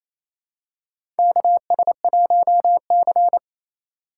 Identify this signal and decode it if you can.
Morse code, heard as a keyed tone, about 28 wpm, 700 Hz. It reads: KS1C
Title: USA Call Signs - 4 Characters - 4x Character Spacing - 28wpm